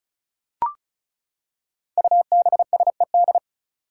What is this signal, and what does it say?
Morse code, heard as a keyed tone, about 34 wpm, 700 Hz. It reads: UBSED
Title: Five Challenging Letter-Number Combinations 35wpm